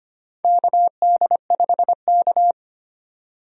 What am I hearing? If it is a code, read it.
Morse code, heard as a keyed tone, about 25 wpm, 700 Hz. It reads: KD5K
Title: USA Call Signs - 4 Characters - 25wpm